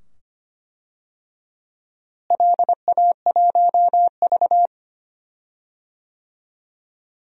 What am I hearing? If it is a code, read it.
Morse code, heard as a keyed tone, about 25 wpm, 700 Hz. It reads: LA1V
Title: DXCC-100 Call Signs - 4 Characters - Repeated 2x - 25wpm